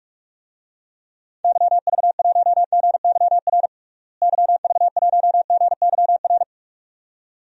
Morse code, heard as a keyed tone, about 41 wpm, 700 Hz. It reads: YV1GYR YV1GYR
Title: DXCC-100 Call Signs - 6 Characters - Repeated 2x - 45wpm